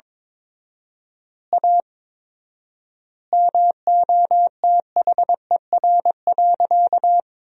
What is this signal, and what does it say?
Morse code, heard as a keyed tone, about 22 wpm, 700 Hz. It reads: A MOTHER.
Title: 4x Word-Spacing - Sentences from Top 200 Words - 22wpm